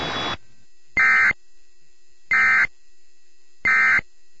Emergency Alert System (SAME) tones, heard as harsh buzzing bursts. It is heard from 0.9 to 4.0 s.